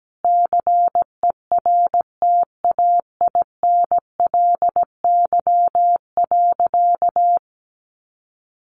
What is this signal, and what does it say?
Morse code, heard as a keyed tone, about 17 wpm, 700 Hz. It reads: CERTAINLY.
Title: Sentences from Top 900 Words 17wpm